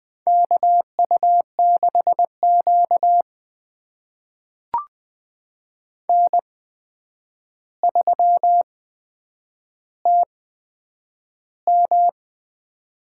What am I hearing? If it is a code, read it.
Morse code, heard as a keyed tone, about 20 wpm, spaced as for 9 wpm, 700 Hz. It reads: KU6Q N3TM
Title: USA Call Signs - 4 Characters - 8x Character Spacing - 20wpm